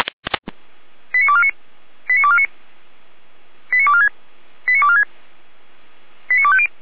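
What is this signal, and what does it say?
Five-tone selective call (ZVEI), heard as quick runs of five tones, tones of about 70 ms. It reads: 89249 89249 89246 89246 89250
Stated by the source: ZVEI five-tone selective call received by SDR.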